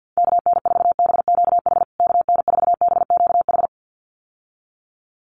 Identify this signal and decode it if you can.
Morse code, heard as a keyed tone, about 50 wpm, 700 Hz. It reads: KN4BQH KN4BQH
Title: USA Call Signs Repeated 2x 50wpm